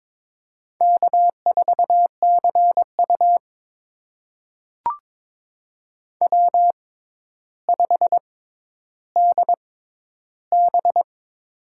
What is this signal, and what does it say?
Morse code, heard as a keyed tone, about 22 wpm, spaced as for 11 wpm, 700 Hz. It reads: K4CU W5DB
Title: USA Call Signs - 6x Character Spacing - 22wpm